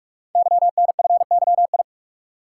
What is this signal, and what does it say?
Morse code, heard as a keyed tone, about 44 wpm, 700 Hz. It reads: YNFYI
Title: Code Groups - 5 Letters - 45wpm